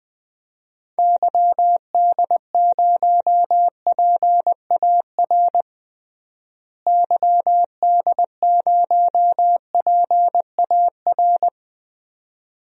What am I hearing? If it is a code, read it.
Morse code, heard as a keyed tone, about 20 wpm, 700 Hz. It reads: YD0PAR YD0PAR
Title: DXCC-100 Call Signs - 6 Characters - Repeated 2x - 20wpm